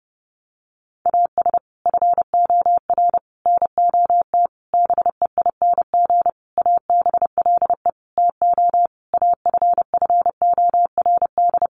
Morse code, heard as a keyed tone, about 28 wpm, 700 Hz. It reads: AS FOR NOT BEING ABLE TO AFFORD
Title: Sentences from Top 1100 Words 30wpm